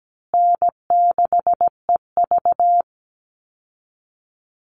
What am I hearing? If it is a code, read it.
Morse code, heard as a keyed tone, about 17 wpm, 700 Hz. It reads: N6EV
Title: Call Signs Repeated 2x - 17wpm